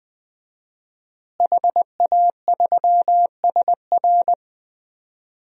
Morse code, heard as a keyed tone, about 20 wpm, 700 Hz. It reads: HA3SR
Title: DXCC-100 Call Signs - 5 Characters - Repeated 3x - 20wpm